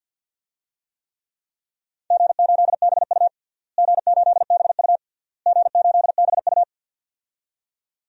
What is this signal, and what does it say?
Morse code, heard as a keyed tone, about 51 wpm, 700 Hz. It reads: G8BU G8BU G8BU
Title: DXCC-100 Call Signs - 4 Characters - Repeated 3x - 50wpm